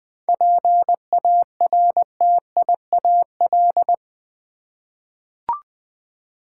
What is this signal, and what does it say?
Morse code, heard as a keyed tone, about 20 wpm, 700 Hz. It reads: PARTIAL
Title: 7 Letter Words - First 3 Letters Given - 20wpm